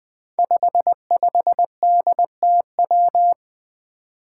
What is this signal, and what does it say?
Morse code, heard as a keyed tone, about 20 wpm, 700 Hz. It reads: 55DTW
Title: Five Challenging Letter-Number Combinations 20wpm